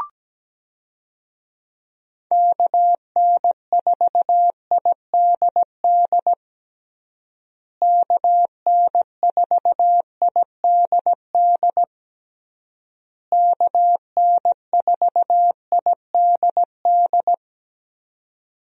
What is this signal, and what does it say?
Morse code, heard as a keyed tone, about 17 wpm, 700 Hz. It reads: KN4IDD KN4IDD KN4IDD
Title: USA Call Signs - 6 Characters - Repeated 3x - 17wpm